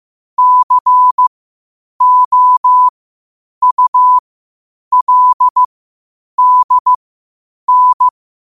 Morse code, heard as a keyed tone, about 15 wpm, spaced as for 8 wpm, 1000 Hz. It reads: COULDN
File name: qotdfiles / 8wpm